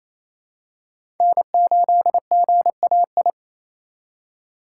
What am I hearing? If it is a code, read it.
Morse code, heard as a keyed tone, about 28 wpm, 700 Hz. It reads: N8GAI
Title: USA Call Signs - 5 Characters - 4x Character Spacing - 28wpm